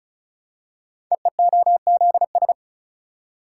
Morse code, heard as a keyed tone, about 34 wpm, 700 Hz. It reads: EEOZS
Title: Code Groups - 5 Letters - 35wpm